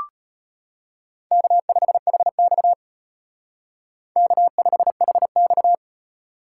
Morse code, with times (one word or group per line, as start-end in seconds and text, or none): 1.3-2.7 K5HX
4.2-5.7 K5HX